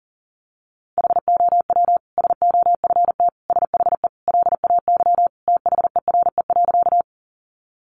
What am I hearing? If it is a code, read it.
Morse code, heard as a keyed tone, about 40 wpm, 700 Hz. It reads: HOW SOFT SHE LAY THERE.
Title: Sentences from Top 1500 Words 40wpm